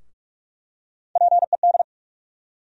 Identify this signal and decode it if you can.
Morse code, heard as a keyed tone, about 46 wpm, 700 Hz. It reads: PED
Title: USA Call Sign Postfixes - 3 Letters 45wpm